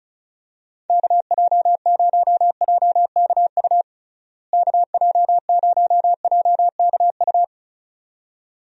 Morse code, heard as a keyed tone, about 34 wpm, 700 Hz. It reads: KJ0JKU KJ0JKU
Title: USA Call Signs - 6 Characters - Repeated 2x - 35wpm